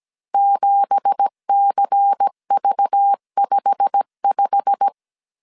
Morse code, heard as a keyed tone, about 17 wpm, 800 Hz. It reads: 7CV55
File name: CW / DET / 1 9 1500C S C 16wpm 2909